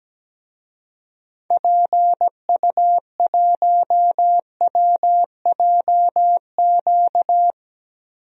Morse code, heard as a keyed tone, about 17 wpm, 700 Hz. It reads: PU1WJQ
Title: DXCC-100 Call Signs - 6 Characters - 17wpm